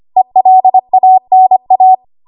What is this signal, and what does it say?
Morse code, heard as a keyed tone, about 25 wpm, 750 Hz. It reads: ELANA
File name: Archive / 25 WPM / 25WPM